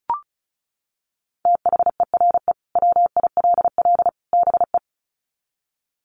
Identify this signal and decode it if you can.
Morse code, heard as a keyed tone, about 34 wpm, 700 Hz. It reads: THERE WILL BE
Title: Sets of 3 Words 35wpm